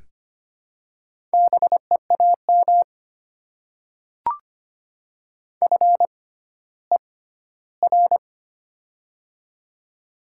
Morse code, heard as a keyed tone, about 25 wpm, spaced as for 14 wpm, 700 Hz. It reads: BEAM FER
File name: Top 90 QSO Elements - 6x Character Spacing - 25wpm